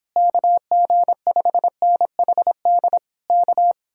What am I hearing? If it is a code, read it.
Morse code, heard as a keyed tone, about 29 wpm, 700 Hz. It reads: KG5NHD K